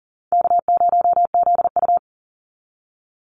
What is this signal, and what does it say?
Morse code, heard as a keyed tone, about 40 wpm, 700 Hz. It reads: K0ZU
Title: Easy Call Signs Repeated 3x - 40wpm